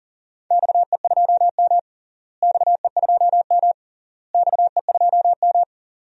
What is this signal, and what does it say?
Morse code, heard as a keyed tone, about 40 wpm, 700 Hz. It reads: XE2M XE2M XE2M
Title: DXCC-100 Call Signs - 4 Characters - Repeated 3x - 40wpm